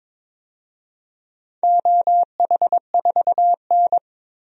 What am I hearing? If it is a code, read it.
Morse code, heard as a keyed tone, about 22 wpm, 700 Hz. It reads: OH4N
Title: DXCC-100 Call Signs - 4 Characters - 22wpm